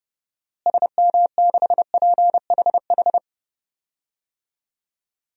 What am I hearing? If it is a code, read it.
Morse code, heard as a keyed tone, about 30 wpm, 700 Hz. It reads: SM6PHH
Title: DXCC-100 Call Signs - 6 Characters - Repeated 2x - 30wpm